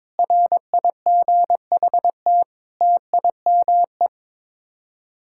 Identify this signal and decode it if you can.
Morse code, heard as a keyed tone, about 22 wpm, 700 Hz. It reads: RIGHT TIME
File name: Sets of 7 Words - 22wpm